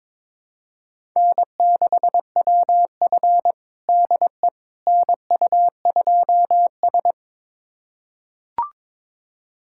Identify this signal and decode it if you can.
Morse code, heard as a keyed tone, about 22 wpm, 700 Hz. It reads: N6WF DE NU2S
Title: USA Callsign DE Callsign - 4 Characters - 22wpm